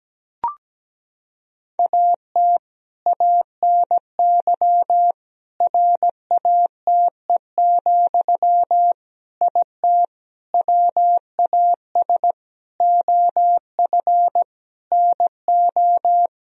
Morse code, heard as a keyed tone, about 17 wpm, 700 Hz. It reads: AT ANY RATE, IT WAS OF NO
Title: Sentences from Top 1000 Words - Review Part 4 - 17wpm